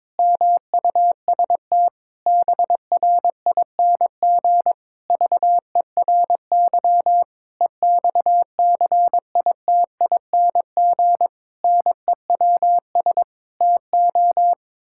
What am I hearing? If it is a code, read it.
Morse code, heard as a keyed tone, about 22 wpm, 700 Hz. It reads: MUST BRING VERY EXCITING NEWS TO
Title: Pangram Challenge v2 - 22wpm